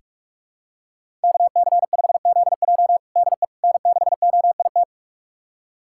Morse code, heard as a keyed tone, about 41 wpm, 700 Hz. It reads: KC5ZJ DE N6OIT
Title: USA Callsign DE Callsign - 5 Characters - 45wpm